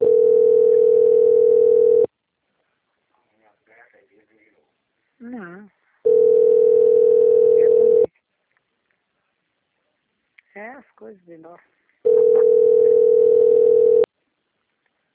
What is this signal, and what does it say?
Ringback tone (North America), 440 + 480 Hz, with speech.
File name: out- - - - - .60853